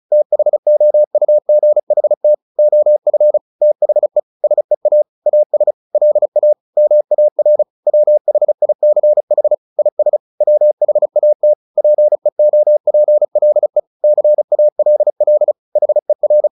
Morse code, heard as a keyed tone, about 34 wpm, 600 Hz. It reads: THOUGHT OF THE SEA AS LA MAR WHICH IS WHAT PEOPLE CALL HER
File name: the old man and the sea / the old man and the sea 1 35wpm